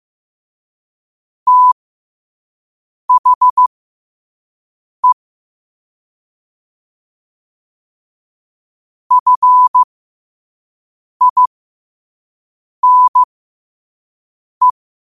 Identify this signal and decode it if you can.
Morse code, heard as a keyed tone, about 15 wpm, spaced as for 5 wpm, 1000 Hz. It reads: THE FINE
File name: qotdfiles / 5wpm